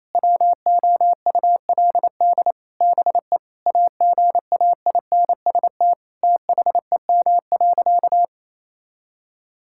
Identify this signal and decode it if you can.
Morse code, heard as a keyed tone, about 28 wpm, 700 Hz. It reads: WOULD BE AGAINST THEM.
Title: Sentences from Top 1000 Words - Review Part 6 - 28wpm